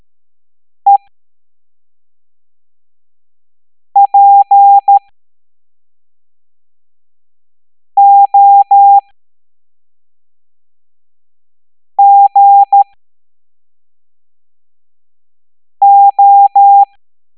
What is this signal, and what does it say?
Morse code, heard as a keyed tone, about 13 wpm, spaced as for 3 wpm, 800 Hz. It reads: EPOGO
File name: hamdata / L07-EX03-HQP